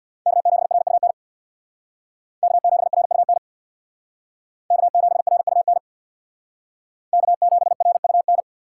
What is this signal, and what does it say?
Morse code, heard as a keyed tone, about 61 wpm, 700 Hz. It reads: K7RUN K7RUN K7RUN K7RUN
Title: Speed Racing - USA 1x3 Call Signs 50wpm